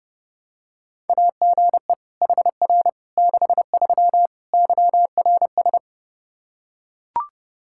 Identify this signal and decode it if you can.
Morse code, heard as a keyed tone, about 30 wpm, 700 Hz. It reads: AGE HR 63 YRS
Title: Practice QSO - Exchange 3 - More Info - 30wpm